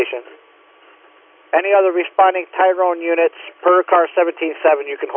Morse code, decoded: none

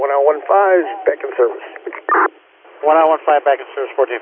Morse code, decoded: none

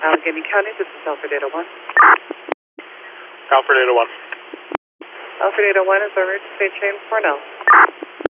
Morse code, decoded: none

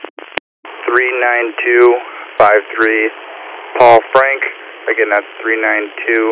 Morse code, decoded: none

